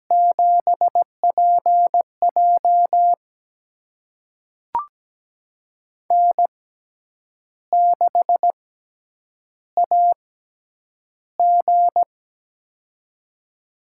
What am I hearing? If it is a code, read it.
Morse code, heard as a keyed tone, about 17 wpm, spaced as for 8 wpm, 700 Hz. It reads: 7PJ N6AG
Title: USA Call Signs - 4 Characters - 6x Character Spacing - 17wpm